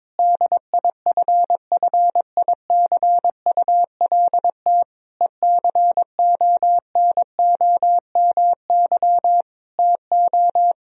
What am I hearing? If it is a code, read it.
Morse code, heard as a keyed tone, about 22 wpm, 700 Hz. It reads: DIFFICULT ECONOMY TO
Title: Sentences from Top 1000 Words - Review Part 3 - 22wpm